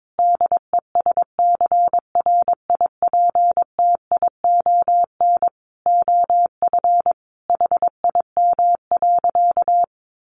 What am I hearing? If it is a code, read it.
Morse code, heard as a keyed tone, about 22 wpm, 700 Hz. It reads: DESCRIPTION OF HIM.
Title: Sentences from Top 1300 Words 22wpm